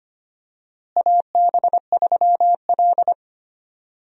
Morse code, heard as a keyed tone, about 25 wpm, 700 Hz. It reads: AB3L
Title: USA Call Signs - 4 Characters - Repeated 2x - 25wpm